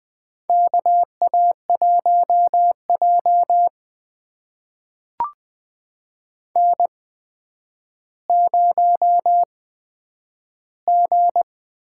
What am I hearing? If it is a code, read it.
Morse code, heard as a keyed tone, about 20 wpm, spaced as for 10 wpm, 700 Hz. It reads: KA1J N0G
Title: USA Call Signs - 8x Character Spacing - 20wpm